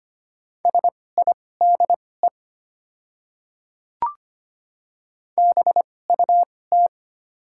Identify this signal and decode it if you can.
Morse code, heard as a keyed tone, about 25 wpm, spaced as for 18 wpm, 690 Hz. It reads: SIDE BUT
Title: 2x Character Spacing - Top 100 Words - 25wpm